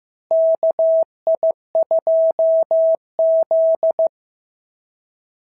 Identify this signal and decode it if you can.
Morse code, heard as a keyed tone, about 14 wpm, 650 Hz. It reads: KI2Z
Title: SOTA QSO Practice 15wpm